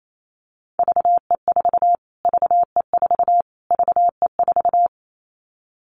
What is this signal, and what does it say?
Morse code, heard as a keyed tone, about 28 wpm, 700 Hz. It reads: VE4 VE4 VE4
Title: Callsign Prefixes for Top 25 DXCC - Repeated 3x - 28wpm